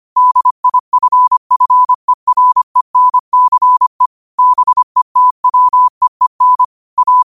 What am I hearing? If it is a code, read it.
Morse code, heard as a keyed tone, about 25 wpm, 1000 Hz. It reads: DIFFERENCE BETWEEN A
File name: qotdfiles / 25wpm